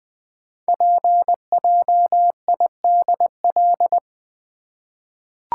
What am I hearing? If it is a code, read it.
Morse code, heard as a keyed tone, about 20 wpm, 700 Hz. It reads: PJIDL
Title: Code Groups - 5 Letters - 20wpm